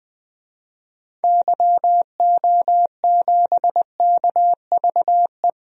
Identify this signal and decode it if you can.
Morse code, heard as a keyed tone, about 20 wpm, 700 Hz. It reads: YO7KVE
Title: DXCC-100 Call Signs - 6 Characters - Repeated 3x - 20wpm